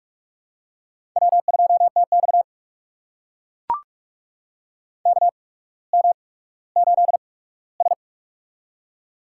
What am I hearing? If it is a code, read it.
Morse code, heard as a keyed tone, about 44 wpm, spaced as for 19 wpm, 690 Hz. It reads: W2TX KM8S
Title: USA Call Signs - 4 Characters - 8x Character Spacing - 45wpm